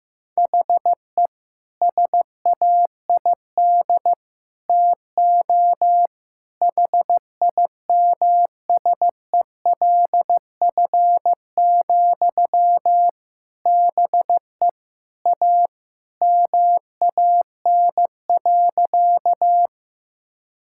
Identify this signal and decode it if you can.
Morse code, heard as a keyed tone, about 15 wpm, 700 Hz. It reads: HE SAID TO HIMSELF, BE A MAN.
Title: Sentences from Top 1600 Words 15wpm